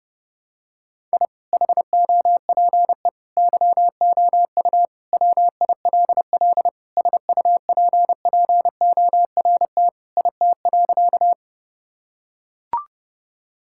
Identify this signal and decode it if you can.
Morse code, heard as a keyed tone, about 32 wpm, 700 Hz. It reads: I HOPE YOU WILL SUPPORT IT.
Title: Sentences from Top 1000 Words - Review Part 5 - 30wpm